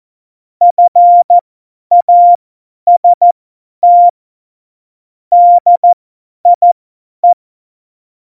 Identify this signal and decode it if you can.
Morse code, heard as a keyed tone, about 14 wpm, spaced as for 10 wpm, 700 Hz. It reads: FAST DIE
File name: Morsekurs / Witze / Witz 13 March 2018, 50 F70